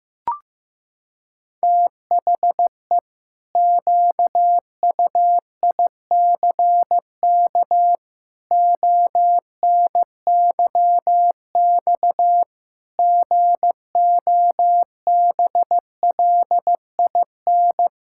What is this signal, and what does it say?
Morse code, heard as a keyed tone, about 15 wpm, 700 Hz. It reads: THE QUICK ONYX GOBLIN
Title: Pangram Challenge v2 - 15wpm